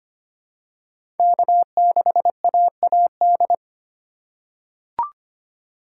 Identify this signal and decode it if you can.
Morse code, heard as a keyed tone, about 25 wpm, 700 Hz. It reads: K6AAD
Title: USA Call Signs - 5 Characters - 2x Character Spacing - 25wpm